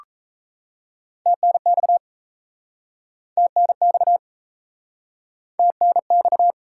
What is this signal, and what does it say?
Morse code, heard as a keyed tone, about 41 wpm, 700 Hz. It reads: TNX TNX TNX